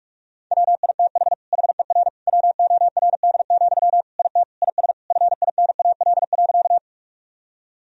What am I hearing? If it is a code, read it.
Morse code, heard as a keyed tone, about 46 wpm, 700 Hz. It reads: WITH HER WORD, IT IS FINAL.
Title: Sentences from Top 1000 Words - Review Part 2 - 45wpm